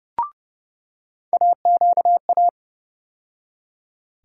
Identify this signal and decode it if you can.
Morse code, heard as a keyed tone, about 30 wpm, 700 Hz. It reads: AQA
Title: USA Call Sign Postfixes - 3 Letters 30wpm